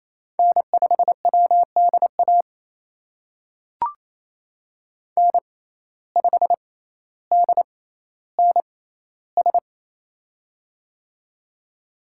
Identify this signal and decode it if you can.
Morse code, heard as a keyed tone, about 28 wpm, spaced as for 14 wpm, 700 Hz. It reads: N5WDA N5DNS
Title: USA Call Signs - 5 Characters - 6x Character Spacing - 28wpm